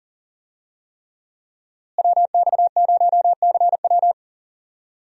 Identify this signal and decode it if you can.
Morse code, heard as a keyed tone, about 40 wpm, 700 Hz. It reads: WX0CW
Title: USA Call Signs - 5 Characters - Repeated 3x - 40wpm